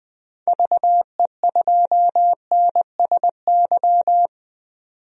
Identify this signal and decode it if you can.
Morse code, heard as a keyed tone, about 20 wpm, 700 Hz. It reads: VE2NSY
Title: DXCC-100 Call Signs - 6 Characters - 20wpm